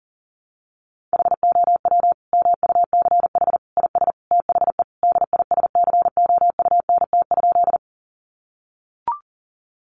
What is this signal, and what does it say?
Morse code, heard as a keyed tone, about 40 wpm, 700 Hz. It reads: HOW MUCH IS THE DISCOUNT?
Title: Sentences from Top 1200 Words 40wpm